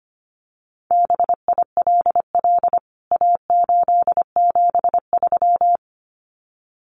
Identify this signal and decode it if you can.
Morse code, heard as a keyed tone, about 25 wpm, 700 Hz. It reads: BILL A873
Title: CWT Exchange 25wpm